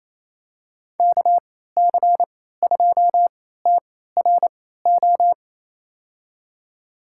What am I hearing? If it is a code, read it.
Morse code, heard as a keyed tone, about 28 wpm, spaced as for 16 wpm, 700 Hz. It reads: KC2TRO
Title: USA Call Signs - 6 Characters - 3x Character Spacing - 28wpm